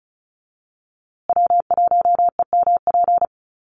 Morse code, heard as a keyed tone, about 34 wpm, 700 Hz. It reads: W1EMP
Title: USA Call Signs Repeated 3x 35wpm